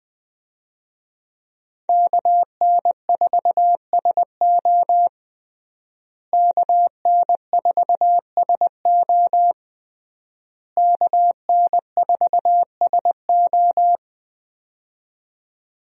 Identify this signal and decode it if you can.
Morse code, heard as a keyed tone, about 20 wpm, 700 Hz. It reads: KN4SO KN4SO KN4SO
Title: USA Call Signs - 5 Characters - Repeated 3x - 20wpm